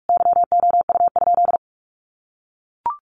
Morse code, heard as a keyed tone, about 44 wpm, 700 Hz. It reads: YOU?
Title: Sentences from Top 1300 Words 45wpm